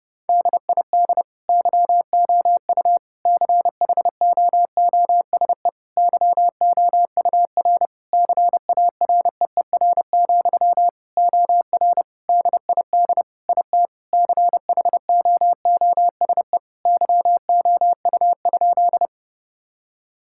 Morse code, heard as a keyed tone, about 30 wpm, 700 Hz. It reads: DID YOU CHOOSE YOUR CAREER, OR DID IT CHOOSE YOU?
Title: Sentences from Top 1000 Words - Review Part 6 - 30wpm